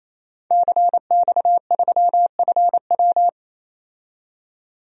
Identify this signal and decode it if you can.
Morse code, heard as a keyed tone, about 28 wpm, 700 Hz. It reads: CX3FW
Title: DXCC-100 Call Signs - 5 Characters - 28wpm